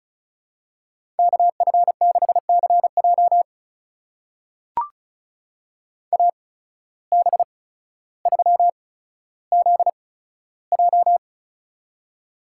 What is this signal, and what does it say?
Morse code, heard as a keyed tone, about 36 wpm, spaced as for 15 wpm, 700 Hz. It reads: KF6CJ AB3ZJ
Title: USA Call Signs - 5 Characters - 8x Character Spacing - 35wpm